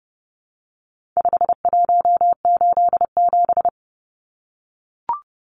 Morse code, heard as a keyed tone, about 30 wpm, 700 Hz. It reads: 5187
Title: Numbers - 4-Digit 30wpm